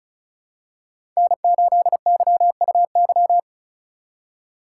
Morse code, heard as a keyed tone, about 34 wpm, 700 Hz. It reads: N8YUY
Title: USA Call Signs - 5 Characters - 2x Character Spacing - 35wpm